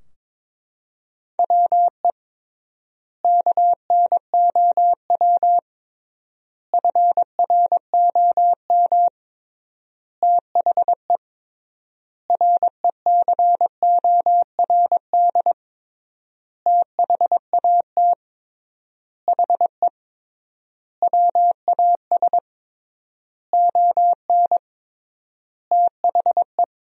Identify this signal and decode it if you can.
Morse code, heard as a keyed tone, about 22 wpm, 700 Hz. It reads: WE KNOW FROM THE RECORD THAT HE WAS ON THE
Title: 3x Word-Spacing - Sentences from Top 500 Words - 22wpm